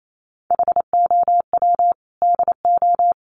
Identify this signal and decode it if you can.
Morse code, heard as a keyed tone, about 28 wpm, 700 Hz. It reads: HOW DO
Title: Sentences from Top 400 Words 28wpm